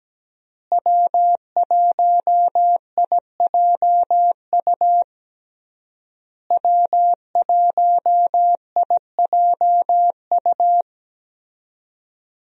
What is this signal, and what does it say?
Morse code, heard as a keyed tone, about 17 wpm, 700 Hz. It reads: W1IJU W1IJU
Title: USA Call Signs - 5 Characters - Repeated 3x - 17wpm